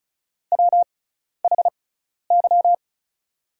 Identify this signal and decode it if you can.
Morse code, heard as a keyed tone, about 34 wpm, spaced as for 12 wpm, 700 Hz. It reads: WHY
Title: 6x Character Spacing - Top 300 Words - 35wpm